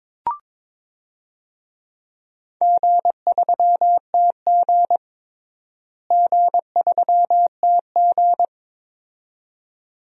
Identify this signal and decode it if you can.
Morse code, heard as a keyed tone, about 22 wpm, 700 Hz. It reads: G3TG G3TG
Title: DXCC-100 Call Signs - 4 Characters - Repeated 2x - 22wpm